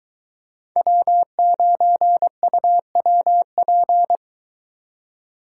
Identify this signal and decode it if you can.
Morse code, heard as a keyed tone, about 22 wpm, 700 Hz. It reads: W9UWP